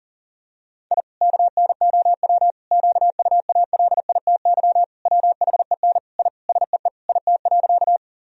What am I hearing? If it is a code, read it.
Morse code, heard as a keyed tone, about 40 wpm, 700 Hz. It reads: I KNOW QUALITY WHEN I SEE IT.
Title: Sentences from Top 1000 Words - Review Part 4 - 40wpm